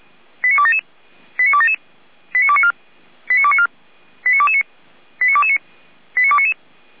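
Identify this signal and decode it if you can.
Five-tone selective call (ZVEI), heard as quick runs of five tones, tones of about 70 ms. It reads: 89277 89277 89294 89294 89229 89229 89299
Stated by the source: ZVEI five-tone selective call 89310 received via SDR.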